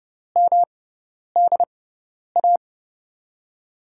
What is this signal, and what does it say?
Morse code, heard as a keyed tone, about 30 wpm, spaced as for 10 wpm, 700 Hz. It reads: MDA
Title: USA Call Signs - 6 Characters - 6x Character Spacing - 30wpm